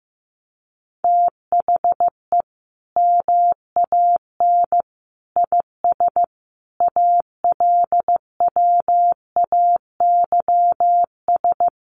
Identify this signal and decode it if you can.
Morse code, heard as a keyed tone, about 15 wpm, 700 Hz. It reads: THE MAN IS ALWAYS
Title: Sentences from Top 1400 Words 15wpm